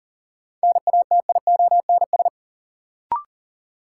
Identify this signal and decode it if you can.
Morse code, heard as a keyed tone, about 37 wpm, 700 Hz. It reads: NATIONS
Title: Sets of 3 Words - New Word Drill - 40wpm